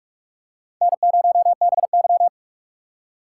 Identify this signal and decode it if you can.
Morse code, heard as a keyed tone, about 44 wpm, 700 Hz. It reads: N0BY
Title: USA Call Signs - 4 Characters - 2x Character Spacing - 45wpm